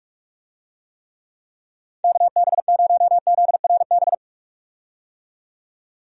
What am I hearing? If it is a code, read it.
Morse code, heard as a keyed tone, about 46 wpm, 700 Hz. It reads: KB0ZRB
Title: USA Call Signs - 6 Characters - 45wpm